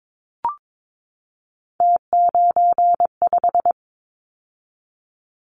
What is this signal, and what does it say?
Morse code, heard as a keyed tone, about 22 wpm, 700 Hz. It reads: T95
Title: Callsign Prefixes for Top 100 DXCC - 22wpm